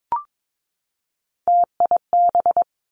Morse code, heard as a keyed tone, about 22 wpm, 700 Hz. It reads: TIB
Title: Three Challenging Letter-Number Combinations 22wpm